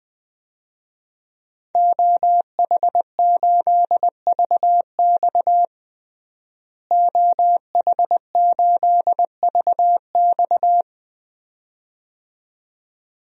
Morse code, heard as a keyed tone, about 21 wpm, 700 Hz. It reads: OH8VX OH8VX
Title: DXCC-100 Call Signs - 5 Characters - Repeated 2x - 20wpm